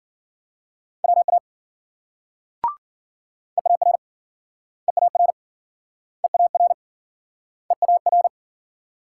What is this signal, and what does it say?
Morse code, heard as a keyed tone, about 52 wpm, 700 Hz. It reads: WA EAR EAR EAR EAR